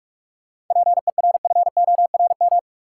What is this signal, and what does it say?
Morse code, heard as a keyed tone, about 41 wpm, 700 Hz. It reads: PERFORM
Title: Sentences from Top 1000 Words - Review Part 5 - New Word Drill - 45wpm